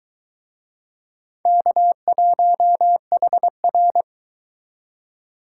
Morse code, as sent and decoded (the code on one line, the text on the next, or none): -.- .---- .... .-.
K1HR